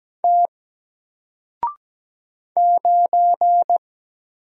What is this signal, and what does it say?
Morse code, heard as a keyed tone, about 17 wpm, 700 Hz. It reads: T 9
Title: Single Letter-Number Alt 17wpm